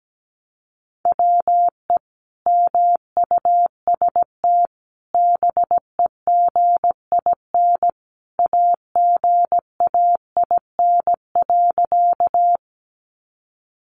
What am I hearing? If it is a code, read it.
Morse code, heard as a keyed tone, about 17 wpm, 700 Hz. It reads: WE MUST BEGIN AGAIN.
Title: Sentences from Top 300 Words 17wpm